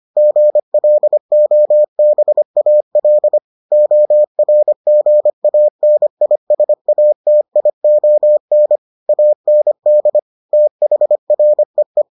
Morse code, heard as a keyed tone, about 25 wpm, 600 Hz. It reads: GLOBAL ORGANISATION AND THREE